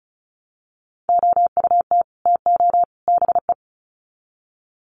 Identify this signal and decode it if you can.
Morse code, heard as a keyed tone, about 34 wpm, 700 Hz. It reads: OUT TO BE
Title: Sets of 3 Words 35wpm